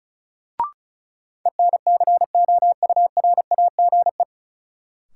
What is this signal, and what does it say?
Morse code, heard as a keyed tone, about 34 wpm, 700 Hz. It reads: ENCOURAGE
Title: Sentences from Top 1400 Words - New Word Drill Alt - 35wpm